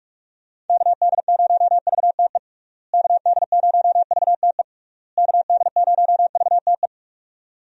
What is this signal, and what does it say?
Morse code, heard as a keyed tone, about 41 wpm, 700 Hz. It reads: KD0VTE KD0VTE KD0VTE
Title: USA Call Signs - 6 Characters - Repeated 3x - 45wpm